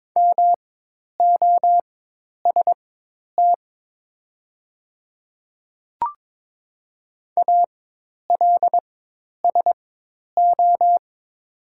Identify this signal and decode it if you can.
Morse code, heard as a keyed tone, about 22 wpm, spaced as for 10 wpm, 700 Hz. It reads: MOST ALSO
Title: 4x Character Spacing - Top 200 Words - 22wpm